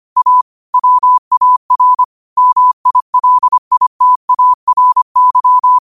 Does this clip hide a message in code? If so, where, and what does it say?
Yes: Morse code from 0.2 to 5.8 s, heard as a keyed tone, about 25 wpm, 1000 Hz. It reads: A WAR MILITARY